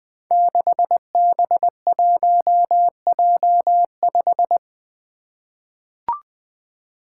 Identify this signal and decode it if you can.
Morse code, heard as a keyed tone, about 20 wpm, 700 Hz. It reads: 6B1J5
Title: Five Challenging Letter-Number Combinations 20wpm